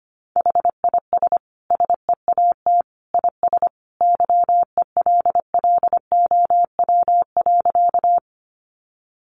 Morse code, heard as a keyed tone, about 25 wpm, 700 Hz. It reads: HIS SEAT IS YELLOW.
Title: Additional Easy Phrases 25wpm